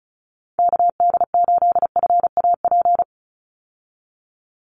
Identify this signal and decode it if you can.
Morse code, heard as a keyed tone, about 34 wpm, 700 Hz. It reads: KD8FAP
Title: USA Call Signs 35wpm